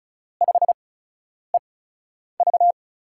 Morse code, heard as a keyed tone, about 36 wpm, 700 Hz. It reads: 5 E V
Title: USA Call Signs - 4 Characters - 8x Character Spacing - 35wpm